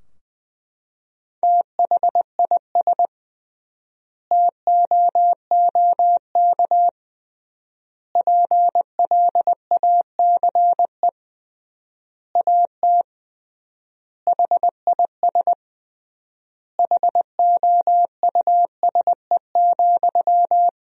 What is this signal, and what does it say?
Morse code, heard as a keyed tone, about 20 wpm, 700 Hz. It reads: THIS TOOK PLACE AT HIS HOUSE,
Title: 3x Word-Spacing - Sentences from Top 300 Words - 20wpm